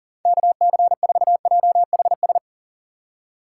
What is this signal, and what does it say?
Morse code, heard as a keyed tone, about 38 wpm, 700 Hz. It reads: KC4JHS
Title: USA Call Signs - 6 Characters - 6x Character Spacing - 40wpm